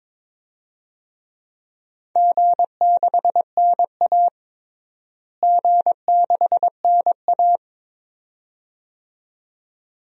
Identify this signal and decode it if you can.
Morse code, heard as a keyed tone, about 22 wpm, 700 Hz. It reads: G6NA G6NA
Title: DXCC-100 Call Signs - 4 Characters - Repeated 2x - 22wpm